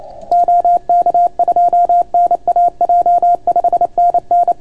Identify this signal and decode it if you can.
Morse code, heard as a keyed tone, about 29 wpm, 690 Hz. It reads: OK2NAJ 5NN